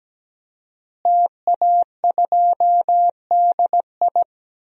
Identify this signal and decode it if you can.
Morse code, heard as a keyed tone, about 17 wpm, 700 Hz. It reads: TA2DI
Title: DXCC-100 Call Signs - 5 Characters - Repeated 2x - 17wpm